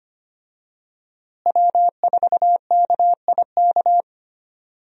Morse code, heard as a keyed tone, about 25 wpm, 700 Hz. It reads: W4KIK
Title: USA Call Signs - 5 Characters - Repeated 2x - 25wpm